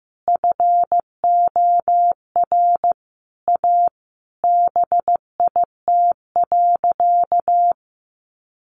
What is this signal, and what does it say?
Morse code, heard as a keyed tone, about 15 wpm, 700 Hz. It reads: FOR A BIT.
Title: Sentences from Top 900 Words 15wpm